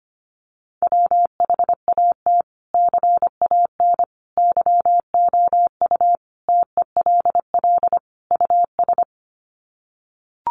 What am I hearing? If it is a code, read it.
Morse code, heard as a keyed tone, about 25 wpm, 700 Hz. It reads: WHAT CAN YOU TELL US
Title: Sets of 5 Words Encore 25wpm